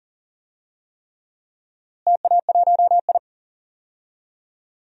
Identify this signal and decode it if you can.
Morse code, heard as a keyed tone, about 40 wpm, 700 Hz. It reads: TA1I
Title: DXCC-100 Call Signs - 4 Characters - Repeated 3x - 40wpm